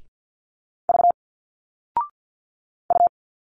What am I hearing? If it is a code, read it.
Morse code, heard as a keyed tone, about 50 wpm, 700 Hz. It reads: V U
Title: Single Letters 50wpm